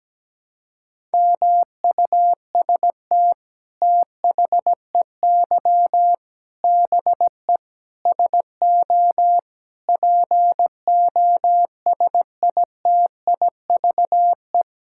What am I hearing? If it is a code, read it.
Morse code, heard as a keyed tone, about 17 wpm, 700 Hz. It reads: MUST THEY BE SO POSITIVE
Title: Sentences from Top 1000 Words - Review Part 5 - 17wpm